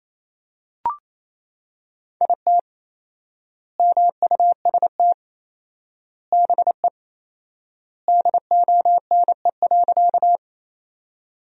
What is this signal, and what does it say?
Morse code, heard as a keyed tone, about 28 wpm, 700 Hz. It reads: IT MUST BE DONE.
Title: 4x Word-Spacing - Sentences from Top 500 Words - 28wpm